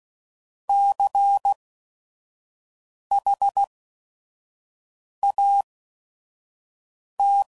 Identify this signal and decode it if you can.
Morse code, heard as a keyed tone, about 16 wpm, spaced as for 5 wpm, 800 Hz. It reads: CHAT
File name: Prefixes / By First Letter / Cs